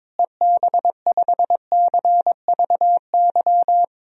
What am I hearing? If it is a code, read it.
Morse code, heard as a keyed tone, about 22 wpm, 700 Hz. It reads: EB5CVY
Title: DXCC-100 Call Signs - 6 Characters - Repeated 2x - 22wpm